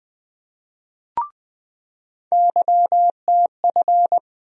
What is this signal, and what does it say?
Morse code, heard as a keyed tone, about 21 wpm, 700 Hz. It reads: YTF
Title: USA Call Sign Postfixes - 3 Letters 20wpm